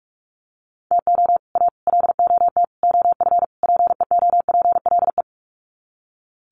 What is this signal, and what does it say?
Morse code, heard as a keyed tone, about 44 wpm, 700 Hz. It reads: TO A LOT OF PEOPLE
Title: Sets of 5 Words Encore 45wpm